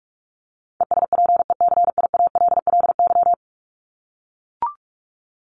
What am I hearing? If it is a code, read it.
Morse code, heard as a keyed tone, about 44 wpm, 700 Hz. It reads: ESPECIALLY
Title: Top 501-750 Words 45wpm